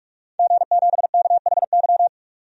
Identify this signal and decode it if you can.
Morse code, heard as a keyed tone, about 43 wpm, 700 Hz. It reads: G7KHY
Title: DXCC-100 Call Signs - 5 Characters - 45wpm